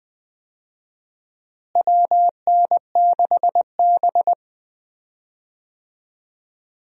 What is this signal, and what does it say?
Morse code, heard as a keyed tone, about 20 wpm, 700 Hz. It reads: WN6B
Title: USA Call Signs - 4 Characters - Repeated 2x - 20wpm